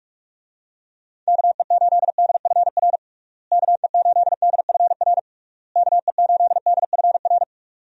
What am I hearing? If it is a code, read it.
Morse code, heard as a keyed tone, about 43 wpm, 700 Hz. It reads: KE8DFR KE8DFR KE8DFR
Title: USA Call Signs - 6 Characters - Repeated 3x - 45wpm